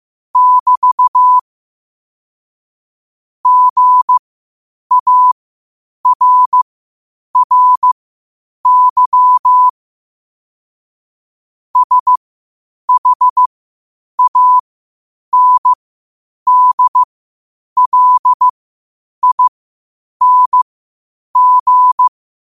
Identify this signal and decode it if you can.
Morse code, heard as a keyed tone, about 15 wpm, spaced as for 8 wpm, 1000 Hz. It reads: <BT> GARRY SHANDLING
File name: qotdfiles / 8wpm